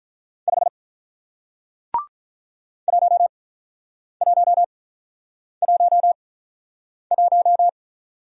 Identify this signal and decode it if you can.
Morse code, heard as a keyed tone, about 48 wpm, spaced as for 9 wpm, 700 Hz. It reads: 5 1111